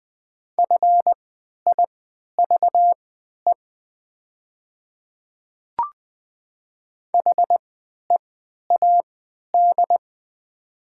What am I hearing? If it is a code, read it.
Morse code, heard as a keyed tone, about 20 wpm, spaced as for 11 wpm, 700 Hz. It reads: FIVE HEAD
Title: 3x Character Spacing - Top 500 Words - 20wpm